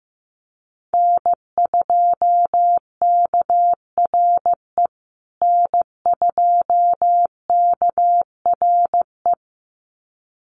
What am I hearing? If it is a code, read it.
Morse code, heard as a keyed tone, about 15 wpm, 700 Hz. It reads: N2KRE N2KRE
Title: USA Call Signs Repeated 2x 15wpm